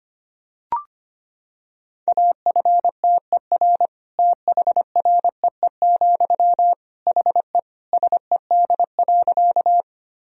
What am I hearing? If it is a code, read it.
Morse code, heard as a keyed tone, about 25 wpm, 700 Hz. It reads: AFTER THREE, HE SED.
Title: Sentences from Top 200 Words and Abbreviations 25wpm